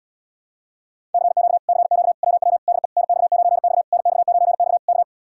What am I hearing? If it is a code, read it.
Morse code, heard as a keyed tone, about 80 wpm, 700 Hz. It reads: CQ CQ WW DE NX9X NX9X K